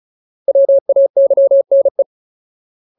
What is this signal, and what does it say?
Morse code, heard as a keyed tone, about 35 wpm, 550 Hz. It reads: WAYNE
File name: Ultimate Word List / 35WPM Ultimate